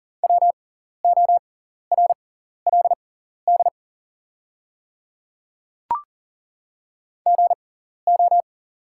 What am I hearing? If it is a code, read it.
Morse code, heard as a keyed tone, about 40 wpm, spaced as for 14 wpm, 700 Hz. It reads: WORLD GO
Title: 6x Character Spacing - Top 200 Words - 40wpm